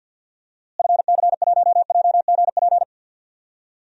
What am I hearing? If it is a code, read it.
Morse code, heard as a keyed tone, about 47 wpm, 700 Hz. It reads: FC1JGP
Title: DXCC-100 Call Signs - 6 Characters - Repeated 3x - 50wpm